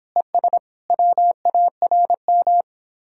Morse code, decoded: ES WARM

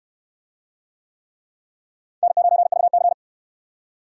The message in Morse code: N9VC

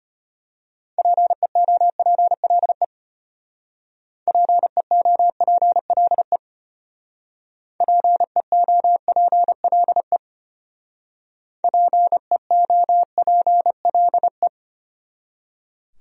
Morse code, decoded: PEOPLE PEOPLE PEOPLE PEOPLE

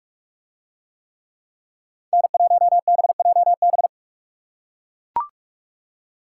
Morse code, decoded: N1BJB

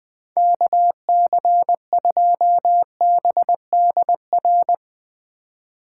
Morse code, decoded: KC2BDR

